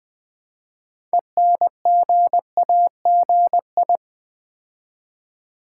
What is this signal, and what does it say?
Morse code, heard as a keyed tone, about 20 wpm, 700 Hz. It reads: ENGAGI
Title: Build A Word - 2 Letters At A Time - 8-Letter Words - 20wpm